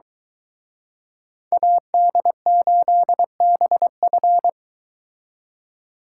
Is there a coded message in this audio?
Yes: Morse code from 1.5 to 4.5 s, heard as a keyed tone, about 23 wpm, 700 Hz. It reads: AD8BF